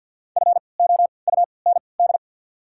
Morse code, heard as a keyed tone, about 53 wpm, spaced as for 29 wpm, 690 Hz. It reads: FOUND
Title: 3x Character Spacing - Top 300 Words - 50wpm